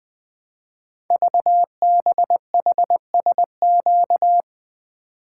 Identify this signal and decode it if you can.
Morse code, heard as a keyed tone, about 20 wpm, 700 Hz. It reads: VBHSQ
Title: Code Groups - 5 Letter:Numbers - 20wpm